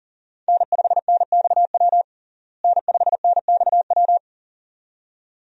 Morse code, heard as a keyed tone, about 40 wpm, 700 Hz. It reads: N5NXW N5NXW
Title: USA Call Signs - 5 Characters - Repeated 2x - 40wpm